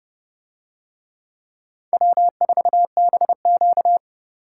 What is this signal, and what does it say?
Morse code, heard as a keyed tone, about 28 wpm, 700 Hz. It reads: W4BQ